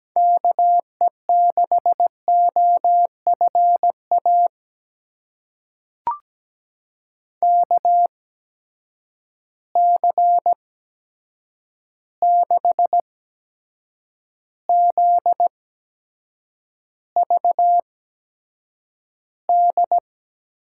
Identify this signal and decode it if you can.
Morse code, heard as a keyed tone, about 17 wpm, spaced as for 7 wpm, 700 Hz. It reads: KE6OFA KC6ZVD
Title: USA Call Signs - 6 Characters - 8x Character Spacing - 17wpm